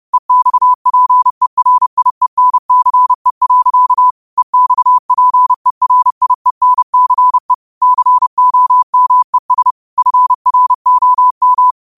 Morse code, heard as a keyed tone, about 30 wpm, 1000 Hz. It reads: EXPERIENCE. EXPERIENCE COMES FROM